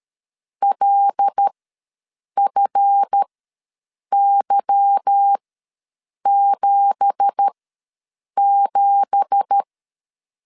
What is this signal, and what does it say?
Morse code, heard as a keyed tone, about 13 wpm, spaced as for 7 wpm, 800 Hz. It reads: LFY77
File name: LEZIONI / 9 2867 ws 12wpm 3431